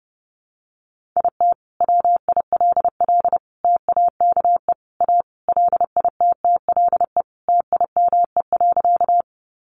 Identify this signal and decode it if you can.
Morse code, heard as a keyed tone, about 30 wpm, 700 Hz. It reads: IT WILL TAKE A LITTLE TIME.
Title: Sentences from Top 200 Words 30wpm